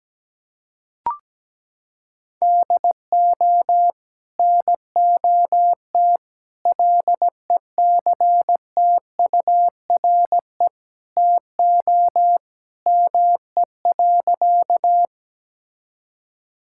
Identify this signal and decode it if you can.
Morse code, heard as a keyed tone, about 17 wpm, 700 Hz. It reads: DO NOT LECTURE TO ME.
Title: Sentences from Top 1600 Words 17wpm